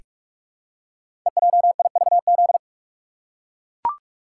Morse code, heard as a keyed tone, about 44 wpm, 700 Hz. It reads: EJIVZ
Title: Code Groups - 5 Letters - 45wpm